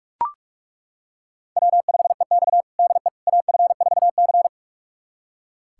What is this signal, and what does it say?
Morse code, heard as a keyed tone, about 41 wpm, 700 Hz. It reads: W5EX DE AF4C
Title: USA Callsign DE Callsign - 4 Characters - 45wpm